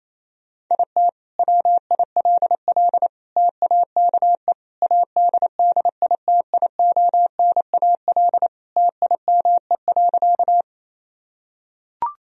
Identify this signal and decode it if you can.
Morse code, heard as a keyed tone, about 28 wpm, 700 Hz. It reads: IT WILL TAKE ADDITIONAL TIME.
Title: Sentences from Top 1000 Words - Review Part 4 - 28wpm